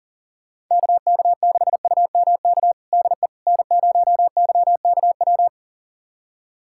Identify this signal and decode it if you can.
Morse code, heard as a keyed tone, about 40 wpm, 700 Hz. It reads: KK6UMK DE N0YKW
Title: USA Callsign DE Callsign 40wpm